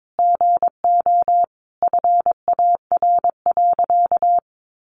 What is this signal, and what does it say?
Morse code, heard as a keyed tone, about 22 wpm, 700 Hz. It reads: GO FAR.
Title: Sentences from Top 400 Words 22wpm